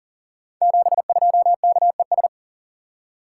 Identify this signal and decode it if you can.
Morse code, heard as a keyed tone, about 39 wpm, 690 Hz. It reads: 72KES
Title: Code Groups - 5 Letter:Numbers - 40wpm